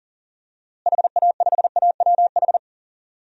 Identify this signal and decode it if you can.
Morse code, heard as a keyed tone, about 40 wpm, 700 Hz. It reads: HA5AWH
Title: DXCC-100 Call Signs - 6 Characters - Repeated 3x - 40wpm